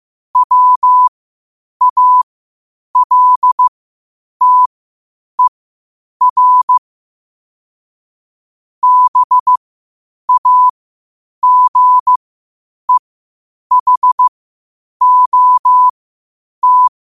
Morse code, heard as a keyed tone, about 15 wpm, spaced as for 8 wpm, 1000 Hz. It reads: WALTER BAGEHOT
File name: qotdfiles / 8wpm